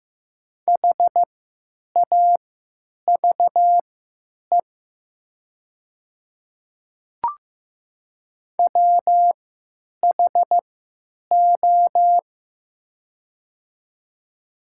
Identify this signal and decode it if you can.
Morse code, heard as a keyed tone, about 15 wpm, spaced as for 9 wpm, 700 Hz. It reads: HAVE WHO
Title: 3x Character Spacing - Top 100 Words - 15wpm